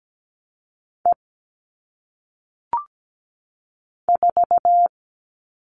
Morse code, heard as a keyed tone, about 17 wpm, 700 Hz. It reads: E 4
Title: All Cut Numbers - 1-Digit 17wpm